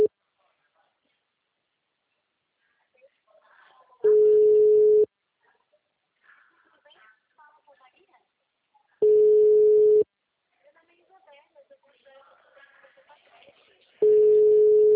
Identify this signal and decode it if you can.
Ringback tone (Europe, 425 Hz), with speech.